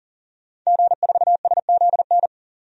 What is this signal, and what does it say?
Morse code, heard as a keyed tone, about 40 wpm, 700 Hz. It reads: G4SZN
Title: DXCC-100 Call Signs - 5 Characters - 40wpm